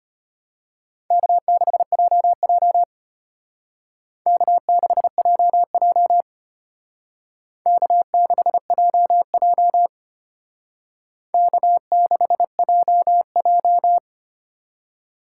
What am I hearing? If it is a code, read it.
Morse code, heard as a keyed tone, about 33 wpm, 700 Hz. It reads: K6JJ K6JJ K6JJ K6JJ